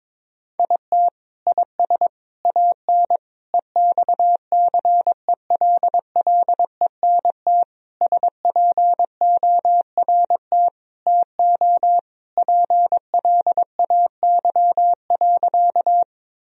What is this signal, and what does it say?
Morse code, heard as a keyed tone, about 21 wpm, 700 Hz. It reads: IT IS AN EXCELLENT SPORT TO PLAY.
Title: Sentences from Top 1000 Words - Review Part 4 - 22wpm